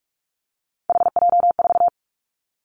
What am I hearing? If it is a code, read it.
Morse code, heard as a keyed tone, about 46 wpm, 700 Hz. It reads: HJ4
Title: Callsign Prefixes for Top 100 DXCC - 45wpm